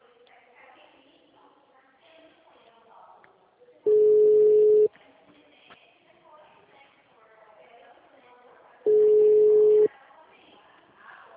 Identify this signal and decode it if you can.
Ringback tone (Europe, 425 Hz), with speech.